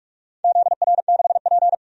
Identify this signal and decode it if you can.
Morse code, heard as a keyed tone, about 41 wpm, 700 Hz. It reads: ZR6P
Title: DXCC-100 Call Signs - 4 Characters - 45wpm